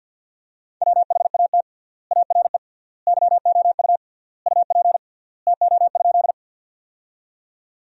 Morse code, heard as a keyed tone, about 50 wpm, 700 Hz. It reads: WHAT ARE YOU UP TO?
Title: 3x Word-Spacing - Sentences from Top 100 Words - 50wpm